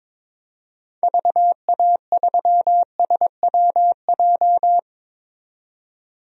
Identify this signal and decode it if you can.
Morse code, heard as a keyed tone, about 22 wpm, 700 Hz. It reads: VA3SWJ
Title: DXCC-100 Call Signs - 6 Characters - 22wpm